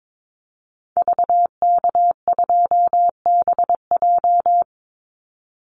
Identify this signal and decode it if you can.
Morse code, heard as a keyed tone, about 22 wpm, 700 Hz. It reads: VK2BJ
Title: Call Signs Repeated 2x - 22wpm